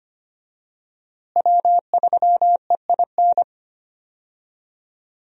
Morse code, heard as a keyed tone, about 25 wpm, 700 Hz. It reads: W3EIN
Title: USA Call Signs - 5 Characters - Repeated 3x - 25wpm